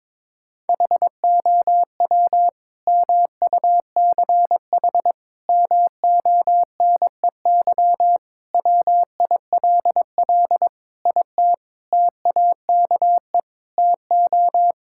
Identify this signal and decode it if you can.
Morse code, heard as a keyed tone, about 22 wpm, 700 Hz. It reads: HOW MUCH MONEY WILL IT TAKE TO
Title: Sentences from Top 1000 Words - Review Part 5 - 22wpm